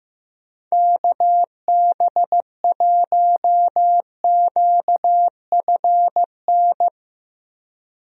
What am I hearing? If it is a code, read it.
Morse code, heard as a keyed tone, about 15 wpm, 700 Hz. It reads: KB1QFN
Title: USA Call Signs - 6 Characters - Repeated 2x - 15wpm